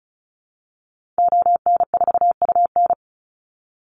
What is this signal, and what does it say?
Morse code, heard as a keyed tone, about 34 wpm, 700 Hz. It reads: ON4UN
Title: Call Signs Repeated 2x - 35wpm